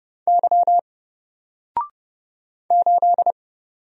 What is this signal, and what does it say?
Morse code, heard as a keyed tone, about 30 wpm, 700 Hz. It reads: Y 8
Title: Single Character (Letter:Number:Punctuation:Prosign) Alt 30wpm